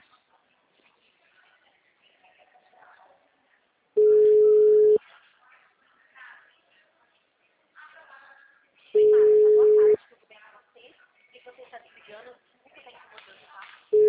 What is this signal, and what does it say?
Ringback tone (Europe, 425 Hz), with speech.